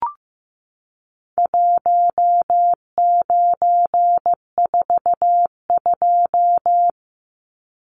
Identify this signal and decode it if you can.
Morse code, heard as a keyed tone, about 15 wpm, 700 Hz. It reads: 1942
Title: Numbers - Contest Style - 4-Digit 15wpm